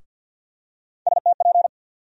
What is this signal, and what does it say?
Morse code, heard as a keyed tone, about 51 wpm, 700 Hz. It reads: STP
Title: USA Call Sign Postfixes - 3 Letters 50wpm